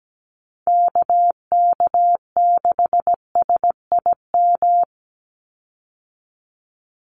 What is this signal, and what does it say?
Morse code, heard as a keyed tone, about 17 wpm, 700 Hz. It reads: KK6SIM
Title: USA Call Signs 17wpm